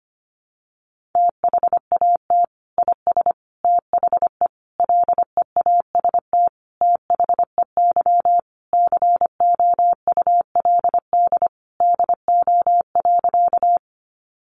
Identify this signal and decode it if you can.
Morse code, heard as a keyed tone, about 25 wpm, 700 Hz. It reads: THAT IS THE LEAST THEY COULD DO.
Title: Sentences from Top 700 Words 25wpm